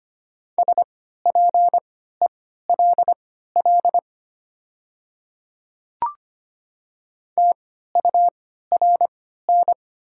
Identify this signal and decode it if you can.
Morse code, heard as a keyed tone, about 25 wpm, spaced as for 14 wpm, 700 Hz. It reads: SPELL TURN
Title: 3x Character Spacing - Top 300 Words - 25wpm